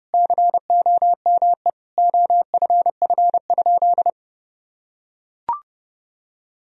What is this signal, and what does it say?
Morse code, heard as a keyed tone, about 29 wpm, 700 Hz. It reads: COME OFF?
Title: Sentences from Top 1000 Words - Review Part 3 - 30wpm